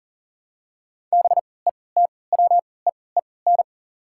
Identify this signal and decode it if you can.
Morse code, heard as a keyed tone, about 40 wpm, spaced as for 23 wpm, 700 Hz. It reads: BETWEEN
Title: 3x Character Spacing - Top 300 Words - 40wpm